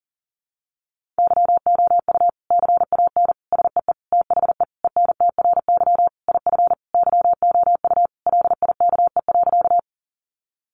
Morse code, heard as a keyed tone, about 37 wpm, 700 Hz. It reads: YOU CAN SEE THE ENTRY IF YOU LIKE.
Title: Sentences from Top 1500 Words 40wpm